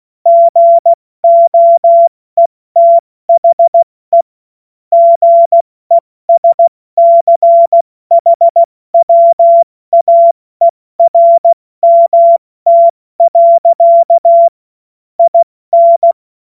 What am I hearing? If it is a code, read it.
Morse code, heard as a keyed tone, about 16 wpm, 680 Hz. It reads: GOETHE GESCHWAERMT. IN
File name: Morsekurs / Witze / Witz 04. Januar 2022, 70 F80